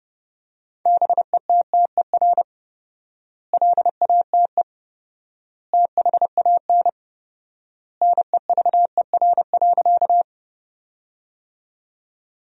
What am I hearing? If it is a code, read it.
Morse code, heard as a keyed tone, about 30 wpm, 700 Hz. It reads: BETTER LATE THAN NEVER.
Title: 4x Word-Spacing - Sentences from Top 400 Words - 30wpm